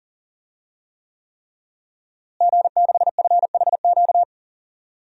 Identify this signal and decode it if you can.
Morse code, heard as a keyed tone, about 40 wpm, 700 Hz. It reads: G6FHQ
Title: DXCC-100 Call Signs - 5 Characters - Repeated 3x - 40wpm